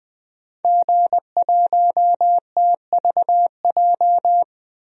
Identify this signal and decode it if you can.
Morse code, heard as a keyed tone, about 20 wpm, 700 Hz. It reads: G1TVJ
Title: DXCC-100 Call Signs - 5 Characters - 20wpm